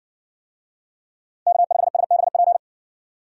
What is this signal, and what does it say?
Morse code, heard as a keyed tone, about 60 wpm, 700 Hz. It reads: K5SBP